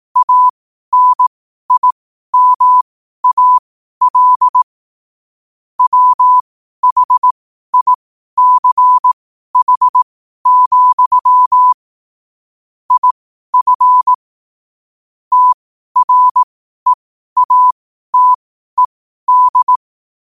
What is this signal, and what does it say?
Morse code, heard as a keyed tone, about 18 wpm, spaced as for 12 wpm, 1000 Hz. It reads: ANIMAL WHICH, IF TREATED
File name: qotdfiles / 12wpm